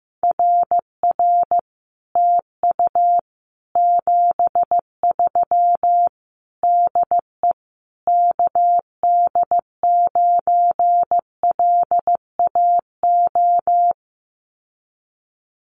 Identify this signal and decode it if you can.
Morse code, heard as a keyed tone, about 15 wpm, 700 Hz. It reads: RR TU 73 DE KD9LAO
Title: Common QSO Phrases Encore 15wpm